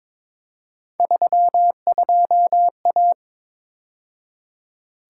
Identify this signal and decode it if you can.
Morse code, heard as a keyed tone, about 27 wpm, 700 Hz. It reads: 32A
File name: Speed-Racing - Three Letter-Number Combinations - 22wpm